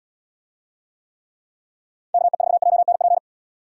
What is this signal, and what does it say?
Morse code, heard as a keyed tone, about 75 wpm, 700 Hz. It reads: KE4JNP